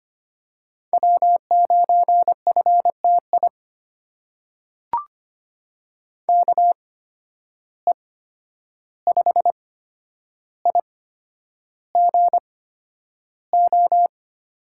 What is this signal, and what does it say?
Morse code, heard as a keyed tone, about 25 wpm, spaced as for 10 wpm, 700 Hz. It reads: W9FTI KE5IGO
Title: USA Call Signs - 8x Character Spacing - 25wpm